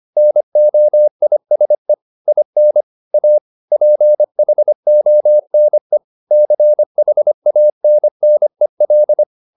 Morse code, heard as a keyed tone, about 25 wpm, 600 Hz. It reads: NOISE IN A PHONE CHANNEL